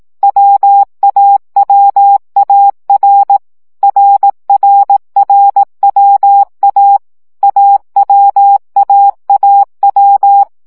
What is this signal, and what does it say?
Morse code, heard as a keyed tone, about 18 wpm, 800 Hz. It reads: WAWAR RRRWA AWAAW
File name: CW gyakorlas a r w 15 baud